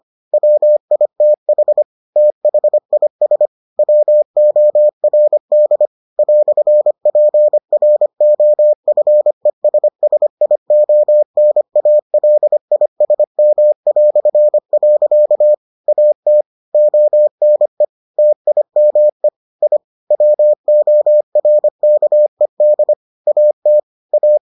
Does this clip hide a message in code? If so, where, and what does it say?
Yes: Morse code from 0.3 to 24.4 s, heard as a keyed tone, about 25 wpm, 600 Hz. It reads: WITH THIS WORD "PROFESSIONALISM". AT ONE TIME I WORKED AT A